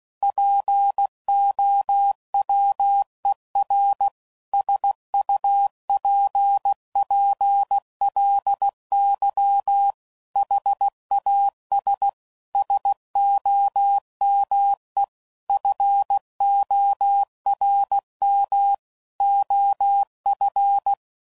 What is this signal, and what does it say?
Morse code, heard as a keyed tone, about 16 wpm, 800 Hz. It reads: POWER SUPPLY HAS SOME FORM OF
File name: CW Code Groups @15WPM / Technical Subjects / PwrSupply